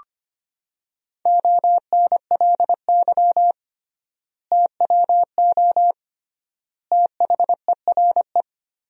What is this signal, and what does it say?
Morse code, heard as a keyed tone, about 25 wpm, 700 Hz. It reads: ONLY TWO THERE
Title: 3x Word-Spacing - Sentences from Top 200 Words - 25wpm